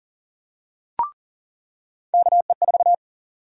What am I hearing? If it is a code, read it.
Morse code, heard as a keyed tone, about 40 wpm, 700 Hz. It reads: KE4
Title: USA Call Sign Prefixes - 40wpm